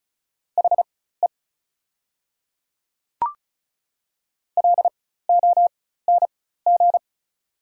Morse code, heard as a keyed tone, about 36 wpm, spaced as for 16 wpm, 690 Hz. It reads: HE LONG
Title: 4x Character Spacing - Top 300 Words - 35wpm